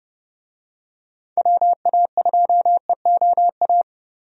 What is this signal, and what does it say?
Morse code, heard as a keyed tone, about 30 wpm, 700 Hz. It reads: WA2EOA
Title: USA Call Signs - 6 Characters - Repeated 3x - 30wpm